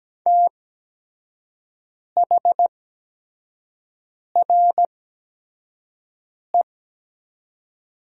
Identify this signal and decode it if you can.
Morse code, heard as a keyed tone, about 17 wpm, spaced as for 5 wpm, 700 Hz. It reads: THRE
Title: 8x Character Spacing - Top 200 Words - 17wpm